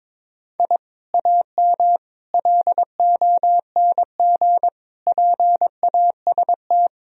Morse code, heard as a keyed tone, about 22 wpm, 700 Hz. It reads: I AM LONG PAST
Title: Sentences from Top 1000 Words - Review Part 3 - 22wpm